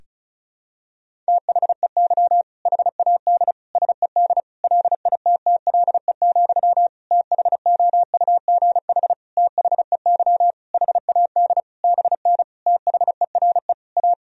Morse code, heard as a keyed tone, about 34 wpm, 700 Hz. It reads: THEY HAD SED LITTLE, THOUGH THEY HAD BN THERE A
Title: Sentences from Top 400 Words and Abbreviations 35wpm